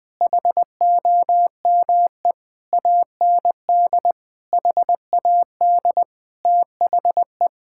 Morse code, heard as a keyed tone, about 21 wpm, 700 Hz. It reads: HOME AND HAD THE
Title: Sentences from Top 1000 Words - Review Part 5 - 20wpm